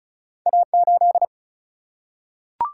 Morse code, heard as a keyed tone, about 36 wpm, 700 Hz. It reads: A8
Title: Single-Letter Single-Number 35wpm